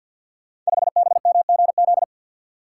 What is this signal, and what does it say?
Morse code, heard as a keyed tone, about 50 wpm, 700 Hz. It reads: 5BMGZ
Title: Code Groups - 5 Letter:Numbers - 50wpm